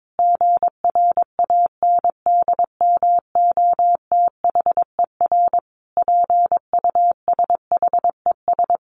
Morse code, heard as a keyed tone, about 22 wpm, 700 Hz. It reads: GRANDMOTHER PUSHES
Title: Additional Easy Phrases 22wpm